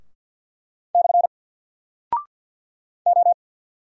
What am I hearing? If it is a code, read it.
Morse code, heard as a keyed tone, about 50 wpm, 700 Hz. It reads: / O
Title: Single Character (Letter:Number:Punctuation:Prosign) 50wpm